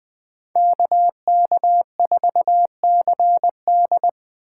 Morse code, heard as a keyed tone, about 24 wpm, 700 Hz. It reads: KK4CD
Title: Speed Racing - USA Call Signs 20wpm